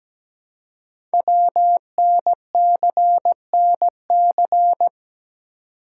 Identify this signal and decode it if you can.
Morse code, heard as a keyed tone, about 17 wpm, 700 Hz. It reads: WNCNC
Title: Code Groups - 5 Letters - 17wpm